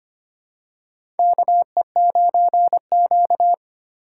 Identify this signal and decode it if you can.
Morse code, heard as a keyed tone, about 25 wpm, 700 Hz. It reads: KE9Q
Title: USA Call Signs - 4 Characters - 4x Character Spacing - 25wpm